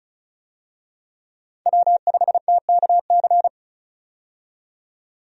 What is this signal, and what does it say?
Morse code, heard as a keyed tone, about 34 wpm, 700 Hz. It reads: W5TKC
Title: USA Call Signs - 5 Characters - 35wpm